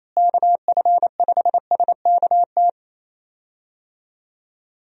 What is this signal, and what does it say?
Morse code, heard as a keyed tone, about 28 wpm, 700 Hz. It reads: KF5SKT
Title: USA Call Signs - 6 Characters - Repeated 2x - 28wpm